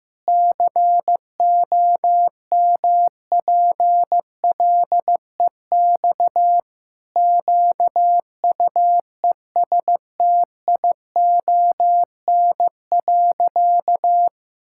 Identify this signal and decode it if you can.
Morse code, heard as a keyed tone, about 15 wpm, 700 Hz. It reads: COMPLEX QUESTION.
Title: Sentences from Top 1000 Words - Review Part 6 - 15wpm